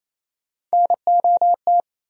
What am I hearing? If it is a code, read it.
Morse code, heard as a keyed tone, about 38 wpm, spaced as for 33 wpm, 700 Hz. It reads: NOT
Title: Speed-Racing - Sets of 5 Words 28wpm